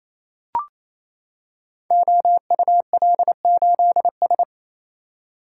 Morse code, heard as a keyed tone, about 28 wpm, 700 Hz. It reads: OUL8S
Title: Code Groups - 5 Letter:Numbers - 28wpm